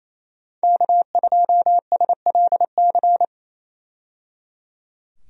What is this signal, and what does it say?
Morse code, heard as a keyed tone, about 28 wpm, 700 Hz. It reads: K2SLC
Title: USA Call Signs - 5 Characters - 28wpm